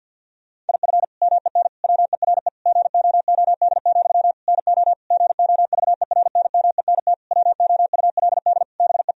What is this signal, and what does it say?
Morse code, heard as a keyed tone, about 51 wpm, 700 Hz. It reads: IF MEN WERE GOOD, NO GOVERNMENT WOULD BE
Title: Sentences from Top 1000 Words - Review Part 3 - 50wpm